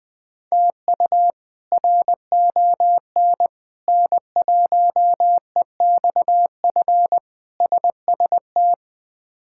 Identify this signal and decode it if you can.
Morse code, heard as a keyed tone, about 20 wpm, 700 Hz. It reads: TU RON N1EXF SST
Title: K1USN SST QSO Practice - 20wpm